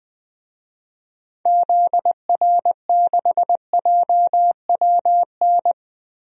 Morse code, heard as a keyed tone, about 21 wpm, 700 Hz. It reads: ZR6JWN
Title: DXCC-100 Call Signs - 6 Characters - Repeated 2x - 20wpm